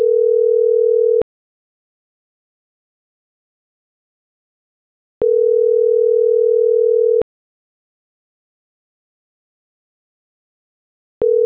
Ringback tone (North America), 440 + 480 Hz.
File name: CENTRAL TERRESTRE / out karis - .6089